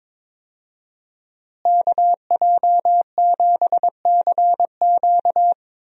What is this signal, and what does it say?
Morse code, heard as a keyed tone, about 22 wpm, 700 Hz. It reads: KJ7CQ
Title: USA Call Signs - 5 Characters - Repeated 3x - 22wpm